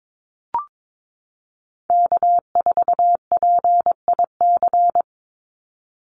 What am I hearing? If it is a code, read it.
Morse code, heard as a keyed tone, about 22 wpm, 700 Hz. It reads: K4PIC
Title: NA SOTA Chaser Callsigns 22wpm